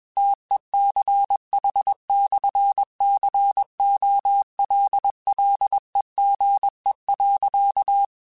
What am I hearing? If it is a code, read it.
Morse code, heard as a keyed tone, about 21 wpm, 800 Hz. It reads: TECH/COLLEGE.
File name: QSOs / QSOs-mp3s / WD0BGP 1@20